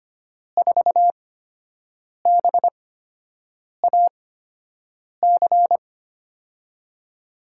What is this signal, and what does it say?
Morse code, heard as a keyed tone, about 25 wpm, spaced as for 7 wpm, 700 Hz. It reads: 4BAC
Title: USA Call Signs - 6 Characters - 8x Character Spacing - 25wpm